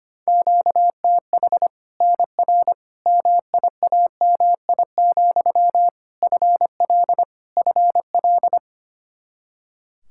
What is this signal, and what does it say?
Morse code, heard as a keyed tone, about 25 wpm, 700 Hz. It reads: QTH NR MIAMI, FL FL
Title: Practice QSO - Exchange 1 - Beginning - 25wpm